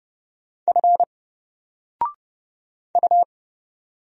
Morse code, heard as a keyed tone, about 31 wpm, 700 Hz. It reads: F U
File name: Single-Letter-Number-30wpm